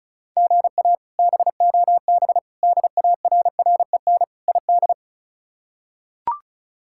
Morse code, heard as a keyed tone, about 34 wpm, 700 Hz. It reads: GA BOB DARREN ID
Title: K1USN SST QSO Practice - 35wpm